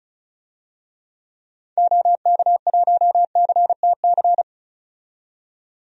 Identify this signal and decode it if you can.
Morse code, heard as a keyed tone, about 36 wpm, 700 Hz. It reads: OK1CTC
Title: DXCC-100 Call Signs - 6 Characters - 35wpm